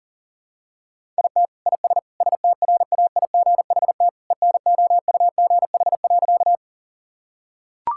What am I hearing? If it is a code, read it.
Morse code, heard as a keyed tone, about 37 wpm, 700 Hz. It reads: IT IS STRAIGHT ENOUGH.
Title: Sentences from Top 1000 Words - Review Part 6 - 40wpm